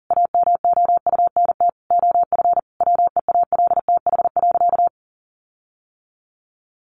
Morse code, heard as a keyed tone, about 40 wpm, 700 Hz. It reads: AMOUNT OF WEALTH.
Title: Sentences from Top 1300 Words 40wpm